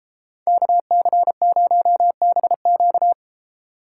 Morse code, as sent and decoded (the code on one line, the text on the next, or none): -.- -.-. ----- -... --.-
KC0BQ